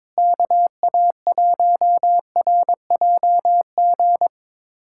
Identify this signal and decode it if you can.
Morse code, heard as a keyed tone, about 22 wpm, 700 Hz. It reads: KA1RJG
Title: USA Callsign DE Callsign 22wpm